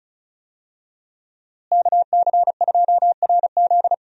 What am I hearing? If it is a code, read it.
Morse code, heard as a keyed tone, about 36 wpm, 700 Hz. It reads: KC2RZ
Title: USA Call Signs - 5 Characters - Repeated 2x - 35wpm